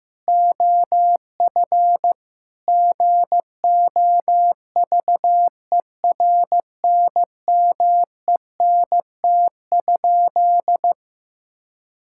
Morse code, heard as a keyed tone, about 15 wpm, 700 Hz. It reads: OF GOVERNMENT?
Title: Sentences from Top 1000 Words - Review Part 5 - 15wpm